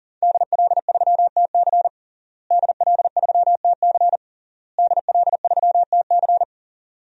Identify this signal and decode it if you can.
Morse code, heard as a keyed tone, about 40 wpm, 700 Hz. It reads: DL3TC DL3TC DL3TC
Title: DXCC-100 Call Signs - 5 Characters - Repeated 3x - 40wpm